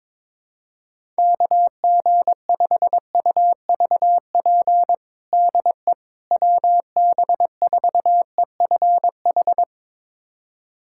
Morse code, heard as a keyed tone, about 22 wpm, 700 Hz. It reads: KG5UVP DE WB4EFH
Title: USA Callsign DE Callsign 22wpm